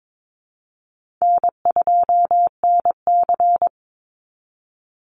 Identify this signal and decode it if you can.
Morse code, heard as a keyed tone, about 22 wpm, 700 Hz. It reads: N2NC
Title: Easy Call Signs Repeated 2x - 22wpm